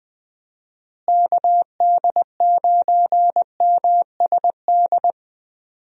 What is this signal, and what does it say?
Morse code, heard as a keyed tone, about 20 wpm, 700 Hz. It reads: KD9MSD
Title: USA Call Signs - 6 Characters - Repeated 2x - 20wpm